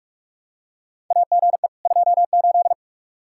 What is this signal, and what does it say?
Morse code, heard as a keyed tone, about 44 wpm, 700 Hz. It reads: AGE 28
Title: K1USN SST QSO Practice 45wpm